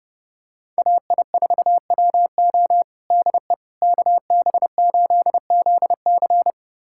Morse code, heard as a keyed tone, about 30 wpm, 700 Hz. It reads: AI4WO DE KB8ZC
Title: USA Callsign DE Callsign - 5 Characters - 30wpm